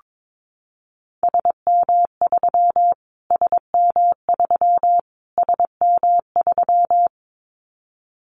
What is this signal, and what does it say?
Morse code, heard as a keyed tone, about 22 wpm, 700 Hz. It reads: SM3 SM3 SM3
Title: Callsign Prefixes for Top 25 DXCC - Repeated 3x - 22wpm